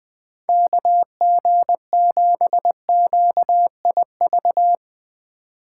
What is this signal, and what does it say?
Morse code, heard as a keyed tone, about 22 wpm, 700 Hz. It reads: KG7QIV
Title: Speed Racing - USA 2x3 Call Signs 20wpm